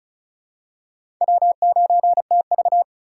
Morse code, heard as a keyed tone, about 36 wpm, 700 Hz. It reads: W9TV
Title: USA Call Signs - 4 Characters - 2x Character Spacing - 35wpm